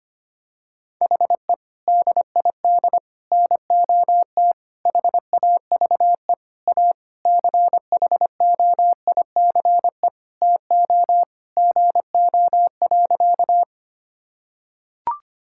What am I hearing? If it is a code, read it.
Morse code, heard as a keyed tone, about 25 wpm, 700 Hz. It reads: HE DID NOT HAVE A CHOICE TO GO.
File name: practice / 25wpm / Sentences with 4-Word Phrases - 25wpm